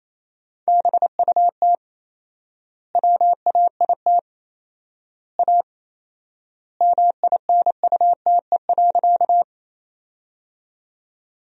Morse code, heard as a keyed tone, about 27 wpm, 700 Hz. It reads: BUT WAIT A MINUTE.
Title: 4x Word-Spacing - Sentences from Top 500 Words - 28wpm